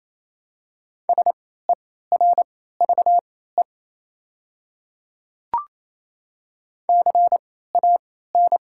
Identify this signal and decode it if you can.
Morse code, heard as a keyed tone, about 28 wpm, spaced as for 16 wpm, 700 Hz. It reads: SERVE CAN
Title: 3x Character Spacing - Top 500 Words - 28wpm